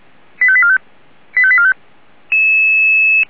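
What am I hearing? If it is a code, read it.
Five-tone selective call (ZVEI), heard as quick runs of five tones, tones of about 70 ms. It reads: 85845 85845
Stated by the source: ZVEI five-tone selective call received with an SDR.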